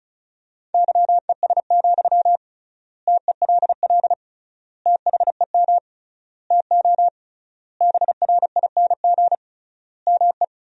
Morse code, heard as a keyed tone, about 34 wpm, 700 Hz. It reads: YES, TELL THEM TO BRING ME
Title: 3x Word-Spacing - Sentences from Top 500 Words - 35wpm